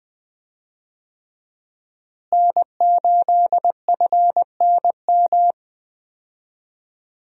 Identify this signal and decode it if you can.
Morse code, heard as a keyed tone, about 21 wpm, 700 Hz. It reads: N8FNM
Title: USA Call Signs - 5 Characters - Repeated 3x - 20wpm